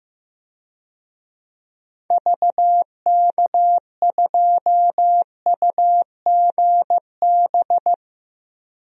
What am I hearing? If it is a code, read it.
Morse code, heard as a keyed tone, about 15 wpm, 700 Hz. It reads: VK2UGB
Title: DXCC-100 Call Signs - 6 Characters - Repeated 3x - 15wpm